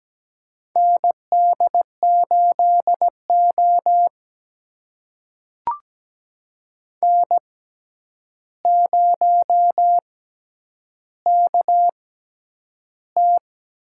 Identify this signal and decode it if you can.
Morse code, heard as a keyed tone, about 17 wpm, spaced as for 9 wpm, 700 Hz. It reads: ND8O N0KT
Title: USA Call Signs - 4 Characters - 6x Character Spacing - 17wpm